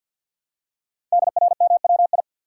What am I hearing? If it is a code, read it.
Morse code, heard as a keyed tone, about 48 wpm, 690 Hz. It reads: DRMWI
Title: Code Groups - 5 Letters - 50wpm